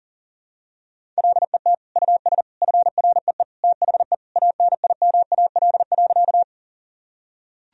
Morse code, heard as a keyed tone, about 40 wpm, 700 Hz. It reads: LET US FREE THE ANIMAL.
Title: Sentences from Top 1000 Words - Review Part 1 - 40wpm